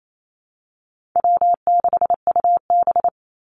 Morse code, heard as a keyed tone, about 28 wpm, 700 Hz. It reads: W6UB
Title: NA SOTA Activator Callsigns 28wpm